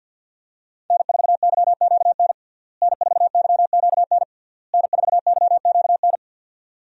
Morse code, heard as a keyed tone, about 53 wpm, 700 Hz. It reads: N4YQN N4YQN N4YQN
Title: USA Call Signs - 5 Characters - Repeated 3x - 50wpm